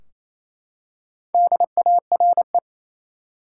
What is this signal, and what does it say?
Morse code, heard as a keyed tone, about 28 wpm, 700 Hz. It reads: DARE
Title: Four Letter Words - 28wpm